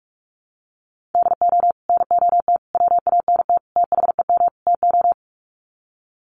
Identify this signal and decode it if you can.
Morse code, heard as a keyed tone, about 44 wpm, 700 Hz. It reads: DO NOT WANT THEM TO
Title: Sets of 5 Words Encore 45wpm